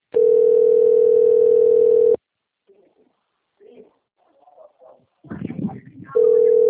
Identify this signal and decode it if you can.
Ringback tone (North America), 440 + 480 Hz, with speech.
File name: rg-1600-unknown- - - .12713